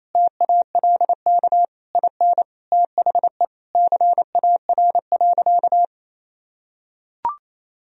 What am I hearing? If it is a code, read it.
Morse code, heard as a keyed tone, about 28 wpm, 700 Hz. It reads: TALK IN THE CAR.
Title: Sentences from Top 1000 Words - Review Part 1 - 28wpm